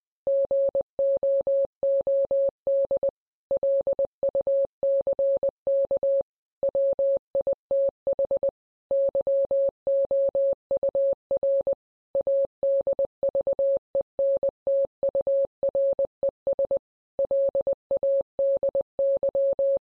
Morse code, heard as a keyed tone, about 20 wpm, 550 Hz. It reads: GOOD LUCK WITH YOUR ADVENTURES LADY